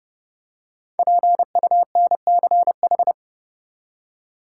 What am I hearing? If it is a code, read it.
Morse code, heard as a keyed tone, about 29 wpm, 700 Hz. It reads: PUNCH
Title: Five Letter Words - 30wpm